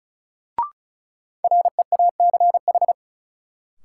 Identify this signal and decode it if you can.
Morse code, heard as a keyed tone, about 36 wpm, 700 Hz. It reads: REACH
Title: Sentences from Top 400 Words - New Word Drill - 35wpm